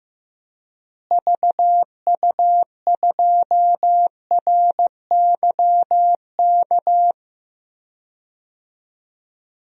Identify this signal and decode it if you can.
Morse code, heard as a keyed tone, about 15 wpm, 700 Hz. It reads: VU2RYK
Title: DXCC-100 Call Signs - 6 Characters - Repeated 3x - 15wpm